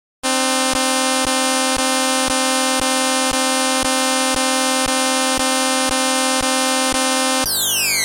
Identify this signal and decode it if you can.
There is no code.